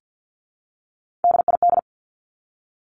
Morse code, heard as a keyed tone, about 48 wpm, 700 Hz. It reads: DID
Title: Top 200 Words 50wpm